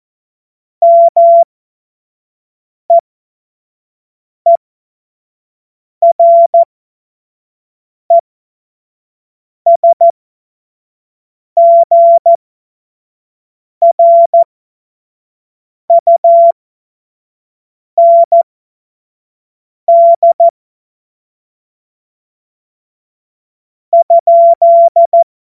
Morse code, heard as a keyed tone, about 14 wpm, spaced as for 5 wpm, 680 Hz. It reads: MEERESGRUND ?
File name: Morsekurs / Witze / Witz 24 Juni 2025 25 F70